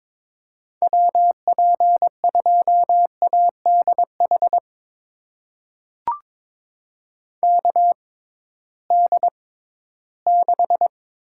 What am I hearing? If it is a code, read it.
Morse code, heard as a keyed tone, about 22 wpm, spaced as for 14 wpm, 700 Hz. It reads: WP2ADH KD6
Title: USA Call Signs - 6 Characters - 6x Character Spacing - 22wpm